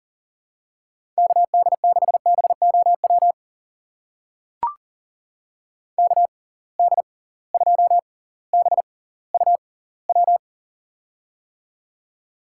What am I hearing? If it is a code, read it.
Morse code, heard as a keyed tone, about 39 wpm, spaced as for 20 wpm, 700 Hz. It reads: KD6BOW KD2BUW
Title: USA Call Signs - 6 Characters - 6x Character Spacing - 40wpm